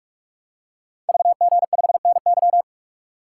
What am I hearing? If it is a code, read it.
Morse code, heard as a keyed tone, about 44 wpm, 700 Hz. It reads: VG5NY
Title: Code Groups - 5 Letter:Numbers - 45wpm